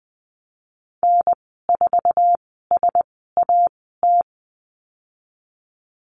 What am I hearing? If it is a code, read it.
Morse code, heard as a keyed tone, about 20 wpm, 700 Hz. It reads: N 4 S A T
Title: 048 Lesson 9g - Number 4 - Call Signs 20wpm